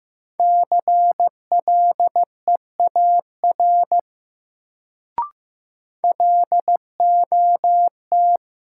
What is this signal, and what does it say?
Morse code, heard as a keyed tone, about 15 wpm, 700 Hz. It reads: CLEAR LOT
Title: Sentences from Top 500 Words - New Word Drill - 15wpm